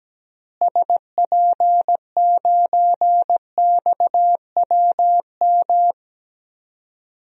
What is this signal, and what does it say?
Morse code, heard as a keyed tone, about 17 wpm, 700 Hz. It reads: SP9XWM
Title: DXCC-100 Call Signs - 6 Characters - Repeated 3x - 17wpm